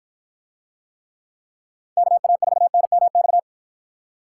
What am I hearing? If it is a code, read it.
Morse code, heard as a keyed tone, about 53 wpm, 700 Hz. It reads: KA4NMX